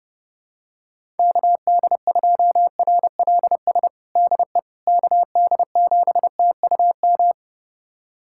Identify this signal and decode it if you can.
Morse code, heard as a keyed tone, about 30 wpm, 700 Hz. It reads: KD2RLS DE KD7TUM
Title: USA Callsign DE Callsign - 6 Characters - 30wpm